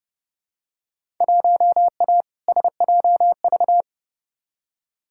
Morse code, heard as a keyed tone, about 30 wpm, 700 Hz. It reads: 1A SJV
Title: Field Day Exchanges Encore - 30wpm